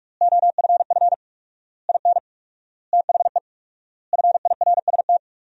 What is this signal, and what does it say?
Morse code, heard as a keyed tone, about 42 wpm, 700 Hz. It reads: OFF IN THE FIRST
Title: 4x Word-Spacing - Sentences from Top 200 Words - 45wpm